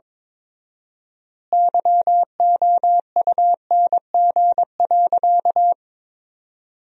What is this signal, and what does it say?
Morse code, heard as a keyed tone, about 22 wpm, 700 Hz. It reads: YOUNG.
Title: 4x Word-Spacing - Sentences from Top 400 Words - 22wpm